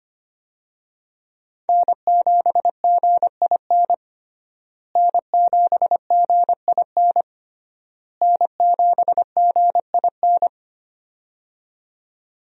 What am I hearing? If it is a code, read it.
Morse code, heard as a keyed tone, about 25 wpm, 700 Hz. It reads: N7GIN N7GIN N7GIN
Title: USA Call Signs - 5 Characters - Repeated 3x - 25wpm